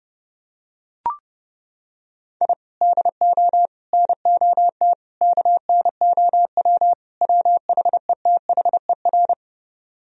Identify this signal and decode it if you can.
Morse code, heard as a keyed tone, about 30 wpm, 700 Hz. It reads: I DO NOT KNOW WHETHER